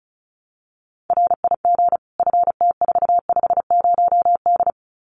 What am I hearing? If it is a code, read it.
Morse code, heard as a keyed tone, about 34 wpm, 700 Hz. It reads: RIG FT450D
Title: Common QSO Phrases 35wpm